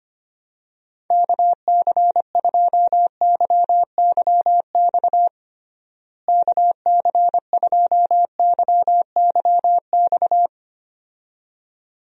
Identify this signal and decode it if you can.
Morse code, heard as a keyed tone, about 25 wpm, 700 Hz. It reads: KC2YYX KC2YYX
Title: USA Call Signs - 6 Characters - Repeated 2x - 25wpm